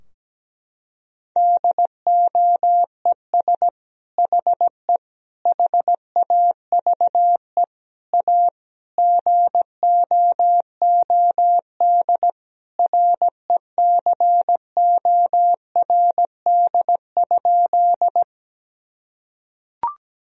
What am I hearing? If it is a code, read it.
Morse code, heard as a keyed tone, about 17 wpm, 700 Hz. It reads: DOES HE HAVE A GOOD RECORD?
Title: Sentences from Top 1000 Words - Review Part 3 - 17wpm